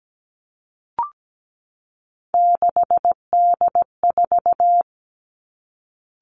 Morse code, heard as a keyed tone, about 17 wpm, 700 Hz. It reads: 6D4
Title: Three Challenging Letter-Number Combinations 17wpm